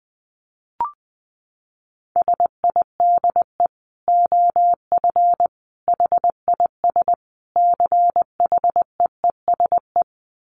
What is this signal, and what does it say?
Morse code, heard as a keyed tone, about 19 wpm, 700 Hz. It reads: SIDE OF HIS CHEESE
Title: Dit Challenge 20wpm